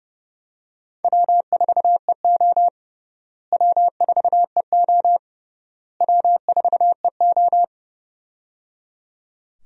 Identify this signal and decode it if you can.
Morse code, heard as a keyed tone, about 29 wpm, 700 Hz. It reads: W4EO W4EO W4EO
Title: USA Call Signs - 4 Characters - Repeated 3x - 30wpm